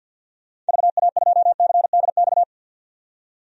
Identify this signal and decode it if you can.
Morse code, heard as a keyed tone, about 47 wpm, 700 Hz. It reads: VA2CDX
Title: DXCC-100 Call Signs - 6 Characters - 50wpm